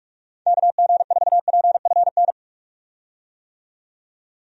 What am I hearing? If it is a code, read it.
Morse code, heard as a keyed tone, about 46 wpm, 700 Hz. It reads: KG4PFN
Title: USA Call Signs - 6 Characters - 45wpm